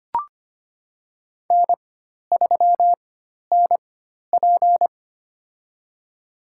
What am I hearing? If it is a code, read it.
Morse code, heard as a keyed tone, about 25 wpm, spaced as for 12 wpm, 700 Hz. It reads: N3NP
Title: USA Call Signs - 4 Characters - 4x Character Spacing - 25wpm